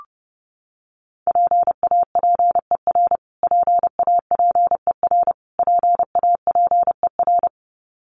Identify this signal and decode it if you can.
Morse code, heard as a keyed tone, about 30 wpm, 700 Hz. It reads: PAPER PAPER PAPER
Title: Top 300 Words Repeated 30wpm